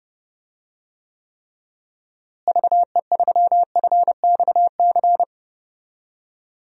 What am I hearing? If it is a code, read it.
Morse code, heard as a keyed tone, about 30 wpm, 700 Hz. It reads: VE3FXC
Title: DXCC-100 Call Signs - 6 Characters - Repeated 3x - 30wpm